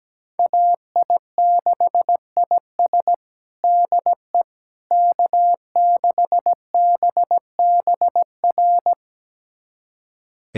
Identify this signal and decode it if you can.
Morse code, heard as a keyed tone, about 17 wpm, 700 Hz. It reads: AI6IS DE K6BBR
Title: USA Callsign DE Callsign - 5 Characters - 17wpm